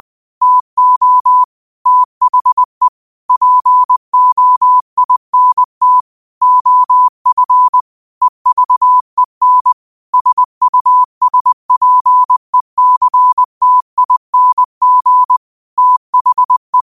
Morse code, heard as a keyed tone, about 20 wpm, 1000 Hz. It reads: TO THE POINT OF EVEN SUSPECTING THE
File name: qotdfiles / 20wpm